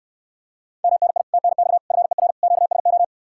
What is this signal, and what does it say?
Morse code, heard as a keyed tone, about 69 wpm, 700 Hz. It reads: MNI TNX FER QSO